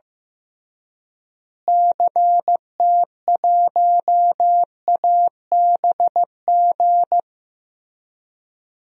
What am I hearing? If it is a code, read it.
Morse code, heard as a keyed tone, about 15 wpm, 700 Hz. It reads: CT1ABG
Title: DXCC-100 Call Signs - 6 Characters - Repeated 2x - 15wpm